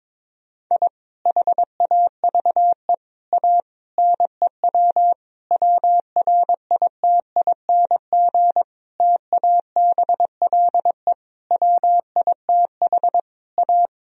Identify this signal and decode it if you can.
Morse code, heard as a keyed tone, about 22 wpm, 690 Hz. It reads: I HAVE A NEW WRITING TABLE WITH A
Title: Sentences from Top 1000 Words - Review Part 3 - 22wpm